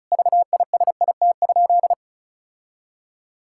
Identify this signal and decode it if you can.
Morse code, heard as a keyed tone, about 36 wpm, 700 Hz. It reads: VISIT?
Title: Sentences from Top 1600 Words 35wpm